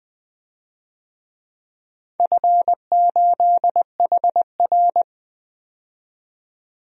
Morse code, heard as a keyed tone, about 21 wpm, 700 Hz. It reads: F8HR
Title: DXCC-100 Call Signs - 4 Characters - 20wpm